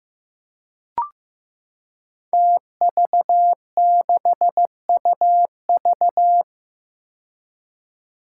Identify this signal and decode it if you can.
Morse code, heard as a keyed tone, about 15 wpm, 700 Hz. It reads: TV6UV
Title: Five Challenging Letter-Number Combinations 15wpm